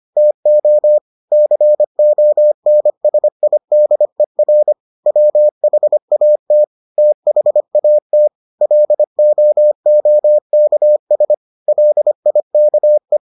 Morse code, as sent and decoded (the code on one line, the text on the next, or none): - --- / -.-. --- -. ... .. -.. .-. / .-- .... .- - / - .... .- - / .-.. --- --- -.- ... / .-.. .. -.- .
TO CONSIDER WHAT THAT LOOKS LIKE